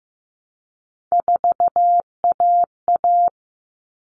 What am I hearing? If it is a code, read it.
Morse code, heard as a keyed tone, about 15 wpm, 700 Hz. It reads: 4AA
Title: All Cut Numbers - 3-Digit 15wpm